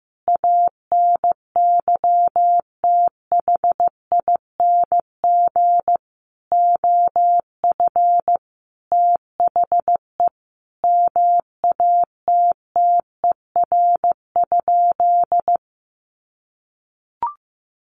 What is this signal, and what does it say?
Morse code, heard as a keyed tone, about 15 wpm, 700 Hz. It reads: ANYTHING OF THE MATTER?
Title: Sentences from Top 1000 Words 15wpm